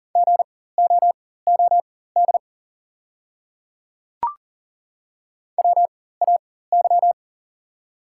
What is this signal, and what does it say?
Morse code, heard as a keyed tone, about 40 wpm, spaced as for 19 wpm, 690 Hz. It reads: GOOD WAY
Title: 4x Character Spacing - Top 200 Words - 40wpm